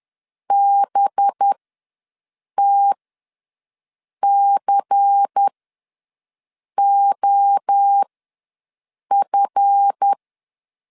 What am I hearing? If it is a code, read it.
Morse code, heard as a keyed tone, about 10 wpm, spaced as for 5 wpm, 800 Hz. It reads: BTCOF
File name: LEZIONI / 8 YJ45 / 8 YJ4 ws 10wpm 4020